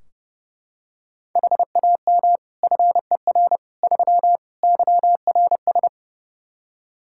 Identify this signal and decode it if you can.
Morse code, heard as a keyed tone, about 30 wpm, 700 Hz. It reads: HAM FER 3 YRS
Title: Practice QSO - Exchange 3 - More Info - 30wpm